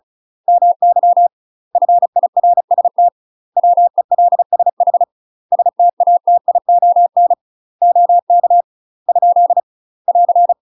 Morse code, heard as a keyed tone, about 34 wpm, 700 Hz. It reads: MY FIRST WELSH STATION OK ? <AR>
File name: UK MASTER QSOs TRAINING/35WPM